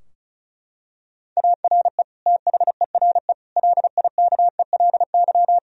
Morse code, heard as a keyed tone, about 36 wpm, 700 Hz. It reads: ARE THERE LIKELY
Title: Sentences from Top 1000 Words - Review Part 4 - 35wpm